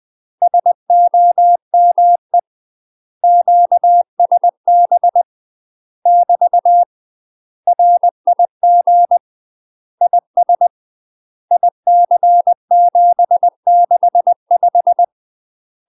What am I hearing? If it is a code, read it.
Morse code, heard as a keyed tone, about 20 wpm, 700 Hz. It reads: SOME QSB <BT> RIG IS IC765
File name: UK MASTER QSOs TRAINING/20WPM